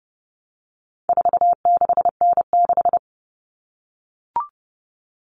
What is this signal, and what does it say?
Morse code, heard as a keyed tone, about 30 wpm, 700 Hz. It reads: 46N6
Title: Common Cut Numbers - 4-Digit 30wpm